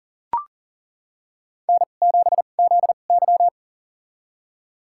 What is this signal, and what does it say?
Morse code, heard as a keyed tone, about 38 wpm, spaced as for 28 wpm, 690 Hz. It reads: N7ZY
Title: USA Call Signs - 4 Characters - 2x Character Spacing - 40wpm